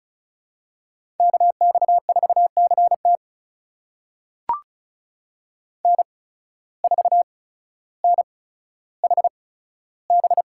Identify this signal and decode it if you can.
Morse code, heard as a keyed tone, about 34 wpm, spaced as for 15 wpm, 700 Hz. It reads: KX4CT N4NHB
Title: USA Call Signs - 5 Characters - 8x Character Spacing - 35wpm